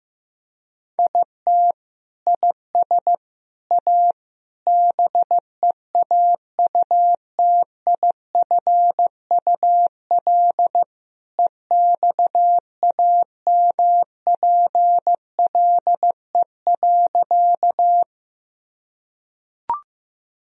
Morse code, heard as a keyed tone, about 15 wpm, 700 Hz. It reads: IT IS A BEAUTIFUL EXAMPLE.
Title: Sentences from Top 1000 Words - Review Part 1 - 15wpm